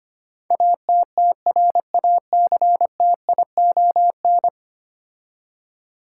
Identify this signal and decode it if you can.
Morse code, heard as a keyed tone, about 25 wpm, 700 Hz. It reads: ATTRACTION
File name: 10 Letter Words - First 3 Letters Given - 25wpm